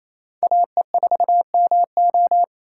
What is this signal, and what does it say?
Morse code, heard as a keyed tone, about 35 wpm, 700 Hz. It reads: AE4MO
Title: Speed Racing - USA 2x2 Call Signs 28wpm